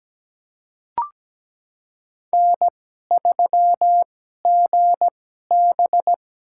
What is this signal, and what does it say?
Morse code, heard as a keyed tone, about 17 wpm, spaced as for 12 wpm, 700 Hz. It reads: N3GB
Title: USA Call Signs - 4 Characters - 2x Character Spacing - 17wpm